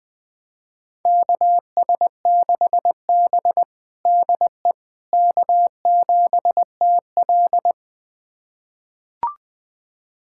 Morse code, heard as a keyed tone, about 20 wpm, 700 Hz. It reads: KS6B DE K7TL
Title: USA Callsign DE Callsign - 4 Characters - 20wpm